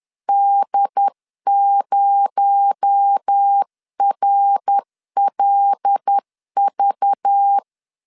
Morse code, heard as a keyed tone, about 10 wpm, 800 Hz. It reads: D0RLV
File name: CW / DET / 1 6 1500C S C 10wpm 4252